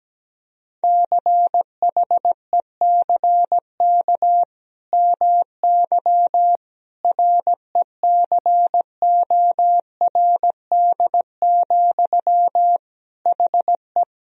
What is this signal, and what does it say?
Morse code, heard as a keyed tone, about 17 wpm, 700 Hz. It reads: CHECK MY RECORD, HE
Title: Sentences from Top 1000 Words - Review Part 3 - 17wpm